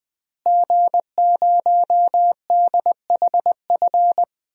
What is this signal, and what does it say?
Morse code, heard as a keyed tone, about 20 wpm, 700 Hz. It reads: G0DHF
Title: DXCC-100 Call Signs - 5 Characters - Repeated 2x - 20wpm